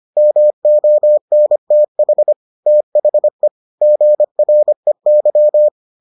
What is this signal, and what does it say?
Morse code, heard as a keyed tone, about 25 wpm, 600 Hz. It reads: MONTH THE GREY